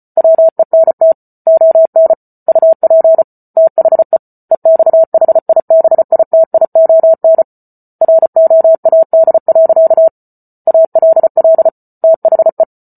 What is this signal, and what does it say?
Morse code, heard as a keyed tone, about 34 wpm, 650 Hz. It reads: WENT ON UP THE EXHIBITION ROAD. ALL THE